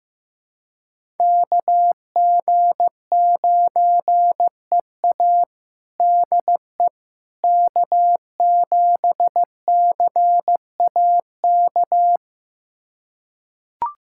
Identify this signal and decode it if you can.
Morse code, heard as a keyed tone, about 15 wpm, 700 Hz. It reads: KG9EA DE K7CAK
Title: USA Callsign DE Callsign - 5 Characters - 15wpm